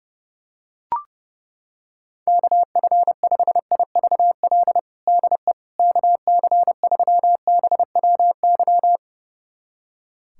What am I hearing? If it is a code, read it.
Morse code, heard as a keyed tone, about 30 wpm, 700 Hz. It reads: KF5IVL DE KC3BWY
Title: USA Callsign DE Callsign - 6 Characters - 30wpm